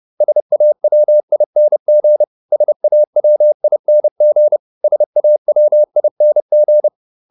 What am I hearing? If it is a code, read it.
Morse code, heard as a keyed tone, about 30 wpm, 600 Hz. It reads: SAWING SAWING SAWING